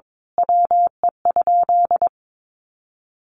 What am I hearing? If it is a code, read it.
Morse code, heard as a keyed tone, about 22 wpm, 700 Hz. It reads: WE?
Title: Sentences from Top 200 Words 22wpm